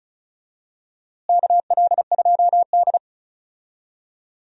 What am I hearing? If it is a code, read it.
Morse code, heard as a keyed tone, about 35 wpm, 700 Hz. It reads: KL2D
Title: USA Call Signs - 4 Characters - Repeated 2x - 35wpm